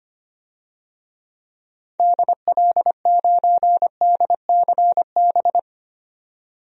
Morse code, heard as a keyed tone, about 25 wpm, 700 Hz. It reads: DL9DCB
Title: DXCC-100 Call Signs - 6 Characters - 25wpm